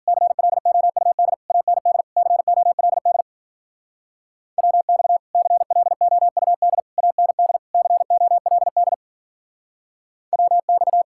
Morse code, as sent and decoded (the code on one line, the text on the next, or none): -.-. .-.. --- ..- -.. / .- -. -.. / -.-. --- .-.. -.. / .-- -..- / -.-. .-.. --- ..- -.. / .- -. -.. / -.-. --- .-.. -.. / .-- -..-
CLOUD AND COLD WX CLOUD AND COLD WX